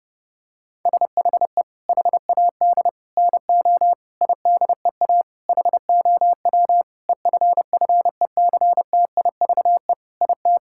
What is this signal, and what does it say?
Morse code, heard as a keyed tone, about 30 wpm, 700 Hz. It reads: SHE HAD NO IDEA HOW EFFECTIVE IT
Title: Sentences from Top 1000 Words - Review Part 6 - 30wpm